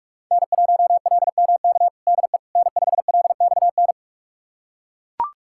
Morse code, heard as a keyed tone, about 43 wpm, 700 Hz. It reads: N1LMK DE N5LXN
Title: USA Callsign DE Callsign - 5 Characters - 45wpm